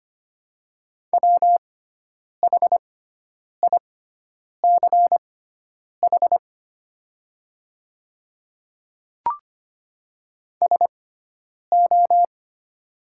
Morse code, heard as a keyed tone, about 25 wpm, spaced as for 9 wpm, 700 Hz. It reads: WHICH SO
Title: 6x Character Spacing - Top 100 Words - 25wpm